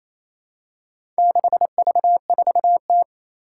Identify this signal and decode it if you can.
Morse code, heard as a keyed tone, about 29 wpm, 700 Hz. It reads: 6V4T
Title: Four Challenging Letter-Number Combinations 28wpm